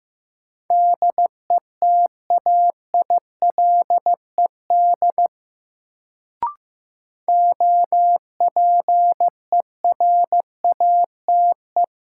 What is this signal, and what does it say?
Morse code, heard as a keyed tone, about 15 wpm, 700 Hz. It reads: DETAILED OPERATE
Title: Sentences from Top 1500 Words - New Word Drill - 15wpm